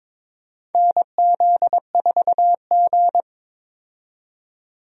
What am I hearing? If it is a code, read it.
Morse code, heard as a keyed tone, about 22 wpm, 700 Hz. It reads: NZ4G
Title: USA Call Signs - 4 Characters - 22wpm